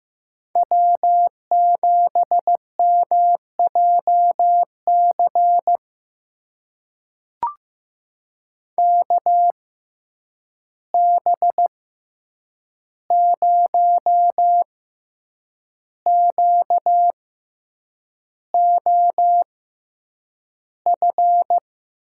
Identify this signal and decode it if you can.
Morse code, heard as a keyed tone, about 15 wpm, spaced as for 7 wpm, 700 Hz. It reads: W7MJC KB0QOF
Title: USA Call Signs - 6x Character Spacing - 15wpm